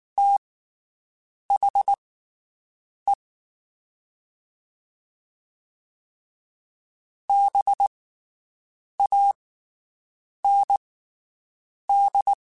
Morse code, heard as a keyed tone, about 19 wpm, spaced as for 7 wpm, 800 Hz. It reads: THE BAND
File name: (spaced) / QSOs Spaced-mp3s / WD0BGP 1 @18 sp